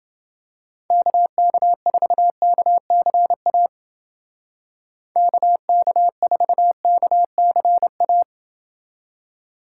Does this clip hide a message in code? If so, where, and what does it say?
Yes: Morse code from 0.9 to 8.2 s, heard as a keyed tone, about 29 wpm, 700 Hz. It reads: KK4KCA KK4KCA